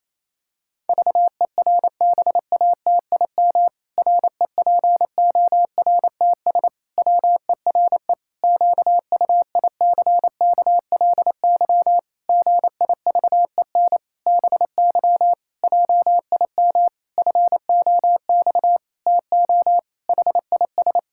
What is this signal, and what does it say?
Morse code, heard as a keyed tone, about 27 wpm, 700 Hz. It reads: VERBATIM REPORTS WERE QUICKLY GIVEN BY JIM FOX TO HIS
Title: Pangram Challenge v2 - 28wpm